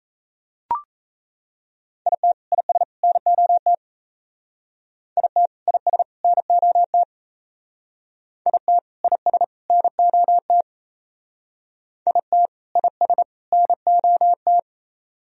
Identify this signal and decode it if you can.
Morse code, heard as a keyed tone, about 35 wpm, 700 Hz. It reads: IT IS NOT IT IS NOT IT IS NOT IT IS NOT
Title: Speed-Racing - Sets of 3 Words 28wpm